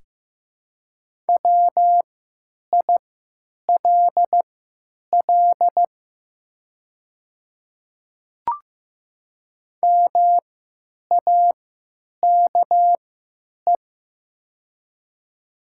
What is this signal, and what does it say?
Morse code, heard as a keyed tone, about 15 wpm, spaced as for 9 wpm, 700 Hz. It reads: WILL MAKE
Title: 3x Character Spacing - Top 100 Words - 15wpm